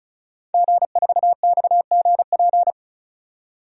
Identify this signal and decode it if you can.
Morse code, heard as a keyed tone, about 34 wpm, 700 Hz. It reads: G4XGP
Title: DXCC-100 Call Signs - 5 Characters - 35wpm